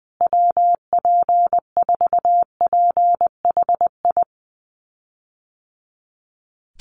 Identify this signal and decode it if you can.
Morse code, heard as a keyed tone, about 20 wpm, 700 Hz. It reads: WP4PHI
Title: USA Call Signs Repeated 3x 20wpm